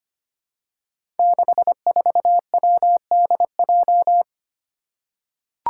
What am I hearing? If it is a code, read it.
Morse code, heard as a keyed tone, about 25 wpm, 700 Hz. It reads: 64WDJ
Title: Five Challenging Letter-Number Combinations 25wpm